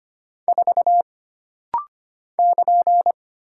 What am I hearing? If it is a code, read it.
Morse code, heard as a keyed tone, about 25 wpm, 700 Hz. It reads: 4 <KN>
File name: Single Character (Letter:Number:Punctuation:Prosign) - Rapid-Fire - 25wpm